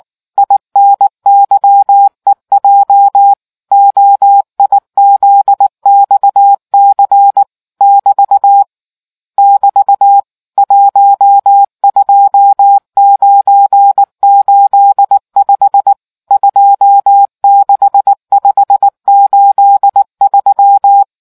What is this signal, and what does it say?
Morse code, heard as a keyed tone, about 19 wpm, 800 Hz. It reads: INYEJ OIZXC <BT> <BT> 12985 26583